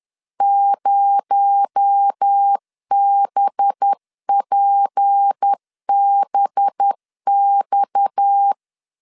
Morse code, heard as a keyed tone, about 10 wpm, 800 Hz. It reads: 0BPBX
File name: LEZIONI / 8 YJ45 / 8 YJ45 10wpm 2549